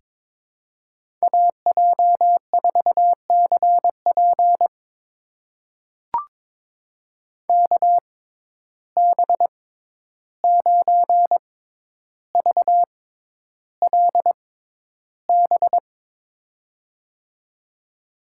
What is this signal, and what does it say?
Morse code, heard as a keyed tone, about 22 wpm, spaced as for 11 wpm, 700 Hz. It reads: AJ4CP KB9VLB
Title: USA Call Signs - 6x Character Spacing - 22wpm